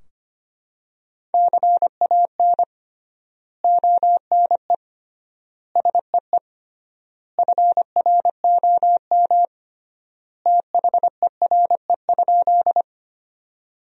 Morse code, heard as a keyed tone, about 25 wpm, 700 Hz. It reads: CAN ONE SEE FROM THERE?
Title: 3x Word-Spacing - Sentences from Top 100 Words - 25wpm